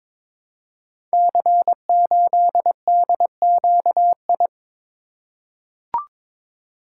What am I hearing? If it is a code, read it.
Morse code, heard as a keyed tone, about 22 wpm, 700 Hz. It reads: C8DQI
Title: Code Groups - 5 Letter:Numbers - 22wpm